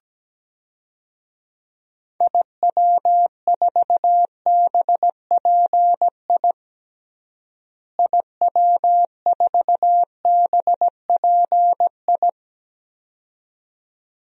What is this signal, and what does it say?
Morse code, heard as a keyed tone, about 17 wpm, 700 Hz. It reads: IW4BPI IW4BPI
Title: DXCC-100 Call Signs - 6 Characters - Repeated 2x - 17wpm